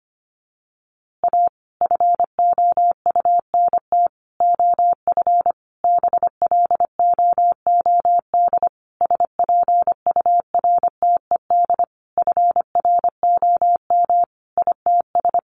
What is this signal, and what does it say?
Morse code, heard as a keyed tone, about 25 wpm, 700 Hz. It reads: A FOUNT OF BLOOD SPURTED FROM ITS
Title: A Sound Of Thunder 25wpm